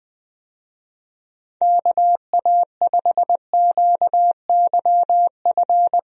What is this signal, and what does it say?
Morse code, heard as a keyed tone, about 20 wpm, 700 Hz. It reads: KA5QYF
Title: USA Call Signs - 6 Characters - 20wpm